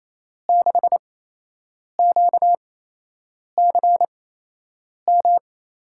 Morse code, heard as a keyed tone, about 27 wpm, spaced as for 8 wpm, 700 Hz. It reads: 6QCM
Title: USA Call Signs - 5 Characters - 8x Character Spacing - 28wpm